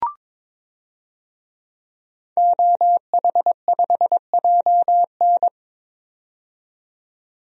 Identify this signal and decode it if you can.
Morse code, heard as a keyed tone, about 22 wpm, 700 Hz. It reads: OH5JN
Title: DXCC-100 Call Signs - 5 Characters - 22wpm